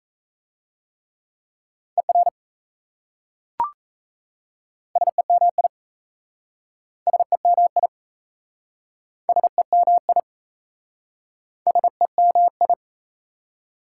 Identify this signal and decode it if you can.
Morse code, heard as a keyed tone, about 38 wpm, 700 Hz. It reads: ER SEMI SEMI SEMI SEMI